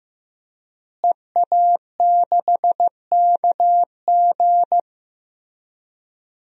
Morse code, heard as a keyed tone, about 15 wpm, 700 Hz. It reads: EA6KG
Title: DXCC-100 Call Signs - 5 Characters - Repeated 2x - 15wpm